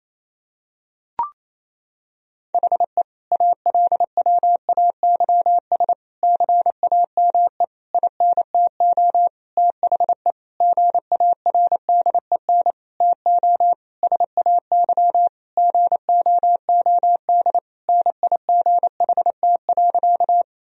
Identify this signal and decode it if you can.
Morse code, heard as a keyed tone, about 28 wpm, 700 Hz. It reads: HE ALWAYS CAME INTO THE GARDEN TO SAY GOOD NIGHT.
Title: Sentences from Top 1000 Words - Review Part 4 - 28wpm